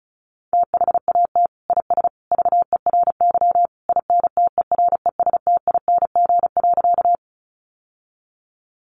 Morse code, heard as a keyed tone, about 34 wpm, 700 Hz. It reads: THAT IS VERY INTERESTING.
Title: Sentences from Top 1000 Words 35wpm